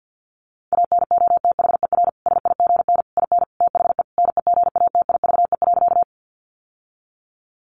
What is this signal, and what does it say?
Morse code, heard as a keyed tone, about 50 wpm, 700 Hz. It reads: ANOTHER SIGN IN THE NEGATIVE.
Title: Sentences from Top 1000 Words 50wpm